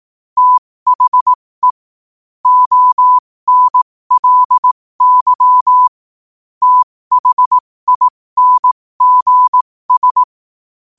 Morse code, heard as a keyed tone, about 18 wpm, spaced as for 15 wpm, 1000 Hz. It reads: THE ONLY THINGS
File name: qotdfiles / 15wpm